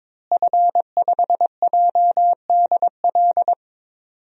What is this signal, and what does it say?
Morse code, heard as a keyed tone, about 22 wpm, 700 Hz. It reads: F5JDL
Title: DXCC-100 Call Signs - 5 Characters - Repeated 2x - 22wpm